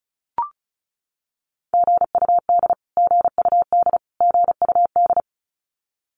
Morse code, heard as a keyed tone, about 34 wpm, 700 Hz. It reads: GUD GUD GUD
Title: Abbreviations Repeated 3x - 35wpm